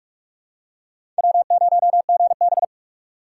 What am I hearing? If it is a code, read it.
Morse code, heard as a keyed tone, about 42 wpm, 700 Hz. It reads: W0GB
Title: USA Call Signs - 4 Characters - 3x Character Spacing - 45wpm